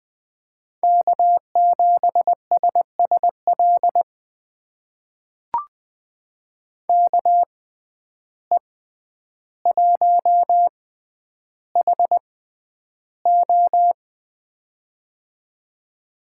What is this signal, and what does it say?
Morse code, heard as a keyed tone, about 20 wpm, spaced as for 10 wpm, 700 Hz. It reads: K7SSL KE1HO
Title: USA Call Signs - 5 Characters - 6x Character Spacing - 20wpm